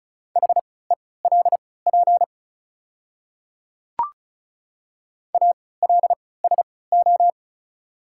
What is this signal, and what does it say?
Morse code, heard as a keyed tone, about 36 wpm, spaced as for 20 wpm, 690 Hz. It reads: HELP ALSO
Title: 3x Character Spacing - Top 200 Words - 35wpm